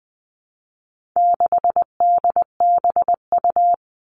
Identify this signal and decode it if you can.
Morse code, heard as a keyed tone, about 20 wpm, 700 Hz. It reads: 6DBU
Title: All Cut Numbers - 4-Digit 20wpm